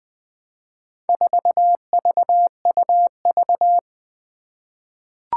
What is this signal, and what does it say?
Morse code, heard as a keyed tone, about 21 wpm, 700 Hz. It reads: 4VUV
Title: Four Challenging Letter-Number Combinations 20wpm